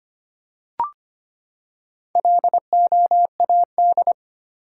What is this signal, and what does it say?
Morse code, heard as a keyed tone, about 25 wpm, 700 Hz. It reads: LOAD
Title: Four Letter Words - 25wpm